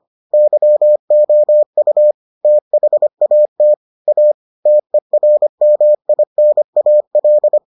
Morse code, heard as a keyed tone, about 25 wpm, 600 Hz. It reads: YOU THAT A TERMINAL